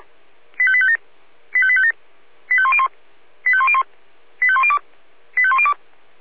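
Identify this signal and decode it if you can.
Five-tone selective call (ZVEI), heard as quick runs of five tones, tones of about 70 ms. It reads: 85857 85857 85101 85101 85102 85102
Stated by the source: Five-tone ZVEI selective call received on SDR.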